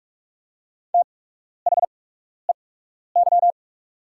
Morse code, heard as a keyed tone, about 44 wpm, spaced as for 12 wpm, 690 Hz. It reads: THEY
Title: 8x Character Spacing - Top 300 Words - 45wpm